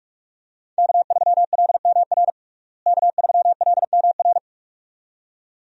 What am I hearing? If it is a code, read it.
Morse code, heard as a keyed tone, about 41 wpm, 700 Hz. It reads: K3LMR K3LMR
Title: USA Call Signs - 5 Characters - Repeated 2x - 45wpm